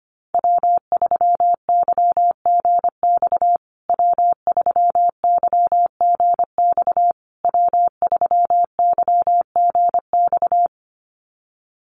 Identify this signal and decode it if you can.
Morse code, heard as a keyed tone, about 25 wpm, 700 Hz. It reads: W3YGX W3YGX W3YGX
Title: USA Call Signs Repeated 3x 25wpm